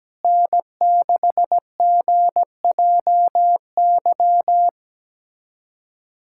Morse code, heard as a keyed tone, about 17 wpm, 700 Hz. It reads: N6GJY
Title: USA Callsign DE Callsign - 5 Characters - 17wpm